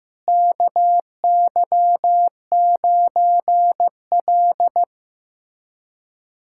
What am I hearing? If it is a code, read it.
Morse code, heard as a keyed tone, about 15 wpm, 700 Hz. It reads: KY9L
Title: USA Callsign DE Callsign 15wpm